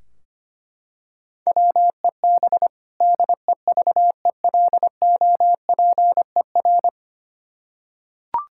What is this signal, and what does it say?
Morse code, heard as a keyed tone, about 25 wpm, 700 Hz. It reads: WEB DEVELOPER
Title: Practice QSO - Exchange 3 - More Info - 25wpm